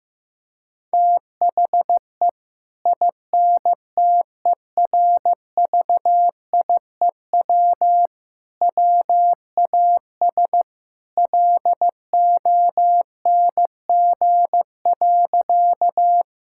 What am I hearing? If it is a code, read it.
Morse code, heard as a keyed tone, about 16 wpm, 700 Hz. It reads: THE INTERVIEW WAS LONG.
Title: Sentences from Top 1600 Words 15wpm